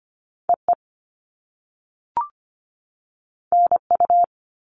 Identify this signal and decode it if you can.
Morse code, heard as a keyed tone, about 25 wpm, 700 Hz. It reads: EE NU
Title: All Cut Numbers - 2-Digit 25wpm